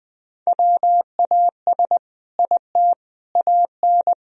Morse code, heard as a keyed tone, about 20 wpm, 700 Hz. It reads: WAS IT AN
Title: Sentences from Top 1000 Words - Review Part 3 - 20wpm